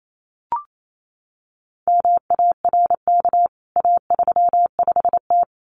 Morse code, heard as a keyed tone, about 28 wpm, 700 Hz. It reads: MARK A35T
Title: CWT Exchange 28wpm